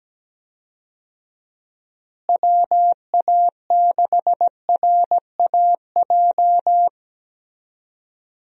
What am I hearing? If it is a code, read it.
Morse code, heard as a keyed tone, about 17 wpm, 700 Hz. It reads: WA6RAJ
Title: USA Call Signs - 6 Characters - Repeated 2x - 17wpm